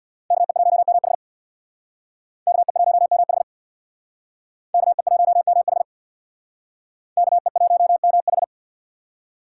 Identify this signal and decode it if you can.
Morse code, heard as a keyed tone, about 61 wpm, 690 Hz. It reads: KE1MH KE1MH KE1MH KE1MH
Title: Speed Racing - USA 2x2 Call Signs 50wpm